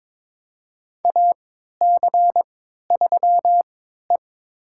Morse code, heard as a keyed tone, about 21 wpm, spaced as for 12 wpm, 700 Hz. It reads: AC3E
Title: USA Call Signs - 4 Characters - 3x Character Spacing - 22wpm